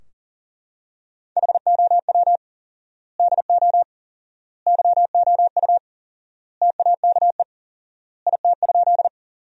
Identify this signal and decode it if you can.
Morse code, heard as a keyed tone, about 40 wpm, 700 Hz. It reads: HOW DO YOU TAKE IT?
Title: 4x Word-Spacing - Sentences from Top 200 Words - 40wpm